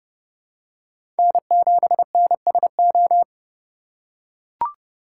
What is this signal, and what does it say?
Morse code, heard as a keyed tone, about 32 wpm, 700 Hz. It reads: N7NSO
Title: USA Call Signs - 5 Characters - 2x Character Spacing - 30wpm